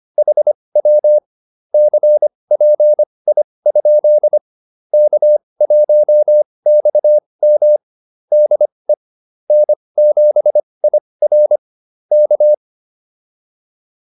Morse code, heard as a keyed tone, about 25 wpm, spaced as for 20 wpm, 600 Hz. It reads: HW CPI? K1XM DE N7IR K
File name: QSO103 20